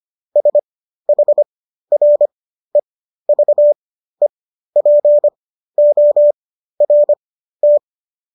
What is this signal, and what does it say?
Morse code, heard as a keyed tone, about 25 wpm, spaced as for 13 wpm, 590 Hz. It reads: SHREVEPORT